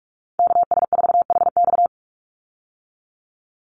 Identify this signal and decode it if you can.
Morse code, heard as a keyed tone, about 44 wpm, 700 Hz. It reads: KS4HX
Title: USA Call Signs 45wpm